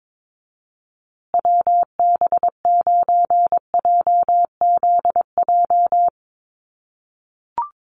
Morse code, heard as a keyed tone, about 22 wpm, 700 Hz. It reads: WB9JZJ
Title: USA Call Signs Repeated 3x 22wpm